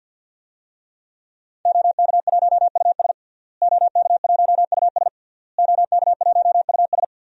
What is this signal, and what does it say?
Morse code, heard as a keyed tone, about 51 wpm, 700 Hz. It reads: OK1US OK1US OK1US
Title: DXCC-100 Call Signs - 5 Characters - Repeated 3x - 50wpm